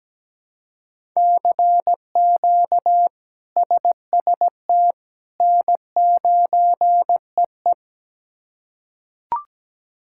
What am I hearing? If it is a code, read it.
Morse code, heard as a keyed tone, about 17 wpm, 700 Hz. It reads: CQ SST N9EE
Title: K1USN SST QSO Practice - 17wpm